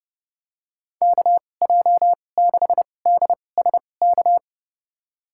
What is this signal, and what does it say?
Morse code, heard as a keyed tone, about 30 wpm, spaced as for 22 wpm, 700 Hz. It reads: KJ6DSK
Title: USA Call Signs - 6 Characters - 2x Character Spacing - 30wpm